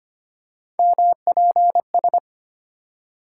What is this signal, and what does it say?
Morse code, heard as a keyed tone, about 25 wpm, 700 Hz. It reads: MPS
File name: Three Letter-Number Combinations - 25wpm